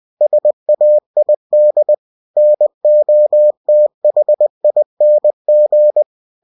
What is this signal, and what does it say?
Morse code, heard as a keyed tone, about 20 wpm, 600 Hz. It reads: SAID NOTHING
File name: cw-books / winnie-the-pooh / winnie-20wpm-page-0015